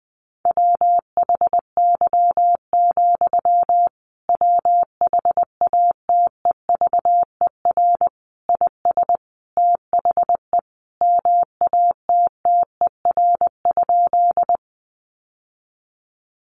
Morse code, heard as a keyed tone, about 20 wpm, 700 Hz. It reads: WHY, WHATEVER IS THE MATTER?
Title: Sentences from Top 1100 Words 20wpm